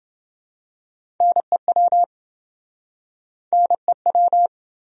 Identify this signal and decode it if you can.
Morse code, heard as a keyed tone, about 28 wpm, 700 Hz. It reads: NEW NEW